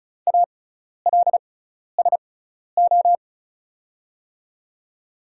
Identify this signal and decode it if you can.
Morse code, heard as a keyed tone, about 35 wpm, spaced as for 12 wpm, 690 Hz. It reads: ALSO
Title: 6x Character Spacing - Top 200 Words - 35wpm